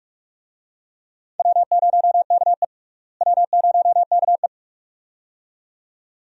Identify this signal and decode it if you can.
Morse code, heard as a keyed tone, about 46 wpm, 700 Hz. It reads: W0KE W0KE
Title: USA Call Signs - 4 Characters - Repeated 2x - 45wpm